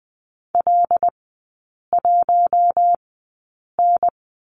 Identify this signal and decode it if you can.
Morse code, heard as a keyed tone, about 20 wpm, spaced as for 8 wpm, 700 Hz. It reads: L1N
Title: 138 Lesson 23e - Letter f - Three Characters 20wpm